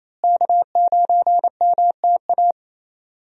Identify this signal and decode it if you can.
Morse code, heard as a keyed tone, about 28 wpm, 700 Hz. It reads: K9MTA
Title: USA Call Signs - 5 Characters - Repeated 2x - 28wpm